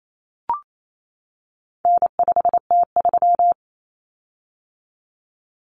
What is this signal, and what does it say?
Morse code, heard as a keyed tone, about 28 wpm, 700 Hz. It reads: N5T3
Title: Common Cut Numbers - 4-Digit 28wpm